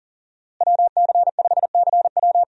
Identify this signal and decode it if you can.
Morse code, heard as a keyed tone, about 39 wpm, 700 Hz. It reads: WC5CW
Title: USA Call Signs - 5 Characters - Repeated 3x - 40wpm